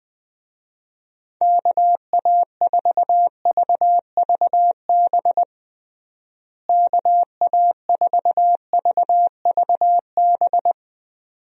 Morse code, heard as a keyed tone, about 21 wpm, 700 Hz. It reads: KA4VVB KA4VVB
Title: USA Call Signs - 6 Characters - Repeated 2x - 20wpm